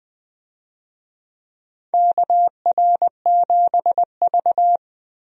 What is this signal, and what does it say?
Morse code, heard as a keyed tone, about 21 wpm, 700 Hz. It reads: KR7V
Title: USA Call Signs - 4 Characters - Repeated 2x - 20wpm